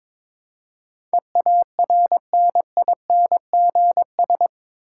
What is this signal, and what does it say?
Morse code, heard as a keyed tone, about 22 wpm, 700 Hz. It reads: EARNINGS
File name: practice / 22wpm / Eight Letter Words - 22wpm